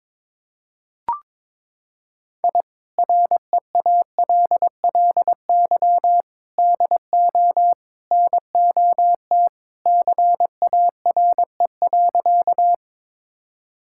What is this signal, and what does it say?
Morse code, heard as a keyed tone, about 22 wpm, 700 Hz. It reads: I REALLY DO NOT CARE.
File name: practice / 22wpm / Sentences with 4-Word Phrases - 22wpm